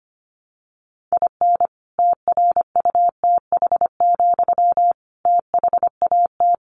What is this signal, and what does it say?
Morse code, heard as a keyed tone, about 25 wpm, 700 Hz. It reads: IN TRUTH, THAT
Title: Sentences from Top 1300 Words 25wpm